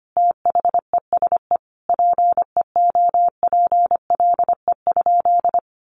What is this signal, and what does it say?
Morse code, heard as a keyed tone, about 25 wpm, 700 Hz. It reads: THESE PEOPLE?
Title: Sentences from Top 200 Words 25wpm